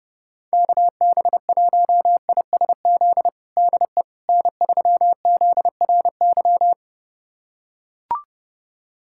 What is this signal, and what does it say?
Morse code, heard as a keyed tone, about 30 wpm, 700 Hz. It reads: KB1ISZ DE N3ZRY
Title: USA Callsign DE Callsign 30wpm